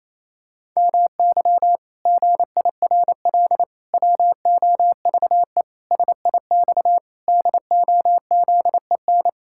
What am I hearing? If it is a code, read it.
Morse code, heard as a keyed tone, about 28 wpm, 700 Hz. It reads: MY GIRL WOVE SIX DOZEN
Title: Pangram Challenge v2 - 28wpm